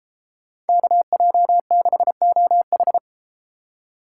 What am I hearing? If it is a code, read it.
Morse code, heard as a keyed tone, about 32 wpm, 700 Hz. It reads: KJ6OH